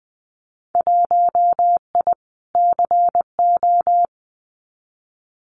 Morse code, heard as a keyed tone, about 20 wpm, 700 Hz. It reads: 1I CO
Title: Winter Field Day Exchanges Encore 20wpm